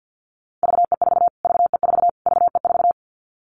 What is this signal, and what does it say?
Morse code, heard as a keyed tone, about 50 wpm, 700 Hz. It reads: VE4 VE4 VE4
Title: Callsign Prefixes for Top 25 DXCC - Repeated 3x - 50wpm